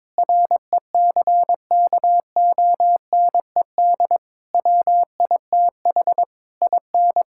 Morse code, heard as a keyed tone, about 22 wpm, 700 Hz. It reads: RECKONED WITH IN
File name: practice / 22wpm / Sets of 8 Words - 22wpm